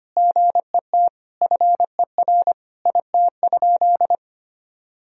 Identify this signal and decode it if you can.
Morse code, heard as a keyed tone, about 25 wpm, 700 Hz. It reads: GET FER IT?
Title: Sentences from Top 400 Words and Abbreviations 25wpm